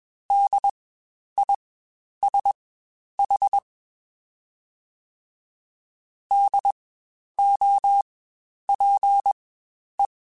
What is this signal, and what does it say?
Morse code, heard as a keyed tone, about 22 wpm, spaced as for 10 wpm, 800 Hz. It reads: DISH DOPE
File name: Spaced Alphas / Spaced Alphas mp3s / Ds sp @20